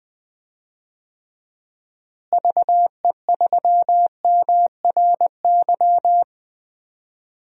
Morse code, heard as a keyed tone, about 20 wpm, 700 Hz. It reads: VE3MRY
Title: DXCC-100 Call Signs - 6 Characters - Repeated 3x - 20wpm